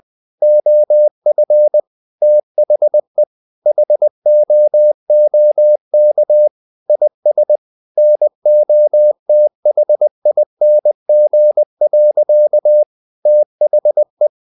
Morse code, heard as a keyed tone, about 20 wpm, 600 Hz. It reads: OF THE HOOK IS NOTHING. THE
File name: the old man and the sea part 2 20wpm